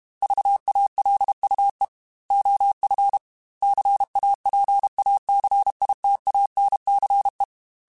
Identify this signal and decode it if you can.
Morse code, heard as a keyed tone, about 32 wpm, 800 Hz. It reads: VALUE OF CAPACITANCE
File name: Technical Subjects / Technical Subjects-mp3s / XC @30